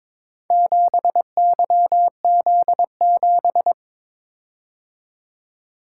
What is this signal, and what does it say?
Morse code, heard as a keyed tone, about 22 wpm, 700 Hz. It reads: 7YZ7
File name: practice / 22wpm / Four Letter-Number Combinations - 22wpm